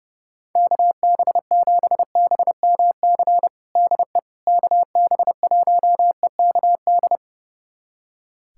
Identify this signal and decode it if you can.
Morse code, heard as a keyed tone, about 30 wpm, 700 Hz. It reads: KB7BMC DE KB1EKD
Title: USA Callsign DE Callsign - 6 Characters - 30wpm